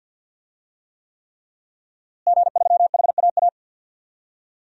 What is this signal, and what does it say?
Morse code, heard as a keyed tone, about 52 wpm, 700 Hz. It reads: G3HAA
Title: DXCC-100 Call Signs - 5 Characters - Repeated 3x - 50wpm